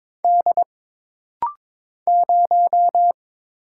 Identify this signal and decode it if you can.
Morse code, heard as a keyed tone, about 22 wpm, 700 Hz. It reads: D 0
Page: This rapid-fire timing is geared to push your ICR (Instant Character Recognition) skills.